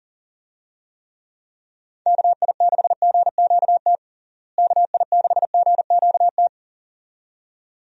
Morse code, heard as a keyed tone, about 37 wpm, 700 Hz. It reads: KI6GQT KI6GQT
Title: USA Call Signs - 6 Characters - Repeated 2x - 40wpm